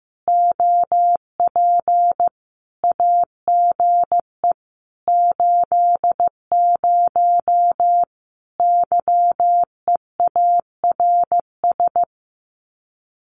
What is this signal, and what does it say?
Morse code, heard as a keyed tone, about 15 wpm, 700 Hz. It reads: OP AGE 80 YEARS
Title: Common QSO Phrases 15wpm